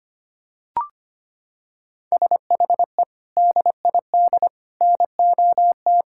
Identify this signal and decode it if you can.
Morse code, heard as a keyed tone, about 25 wpm, 700 Hz. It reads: SHE DID NOT
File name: practice / 25wpm / Sentences with 4-Word Phrases - 25wpm